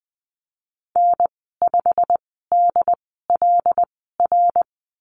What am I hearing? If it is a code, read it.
Morse code, heard as a keyed tone, about 20 wpm, spaced as for 15 wpm, 700 Hz. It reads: N5DLR
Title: 128 Lesson 21g - Number 6 - Call Signs 20wpm